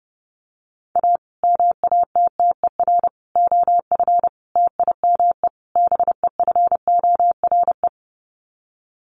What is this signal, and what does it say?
Morse code, heard as a keyed tone, about 28 wpm, 700 Hz. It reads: A MATTER OF TIME BEFORE
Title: Sets of 5 Words Encore 30wpm